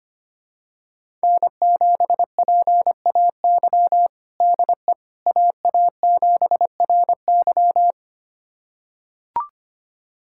Morse code, heard as a keyed tone, about 25 wpm, 700 Hz. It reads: N7PAY DE AA7RY
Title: USA Callsign DE Callsign - 5 Characters - 25wpm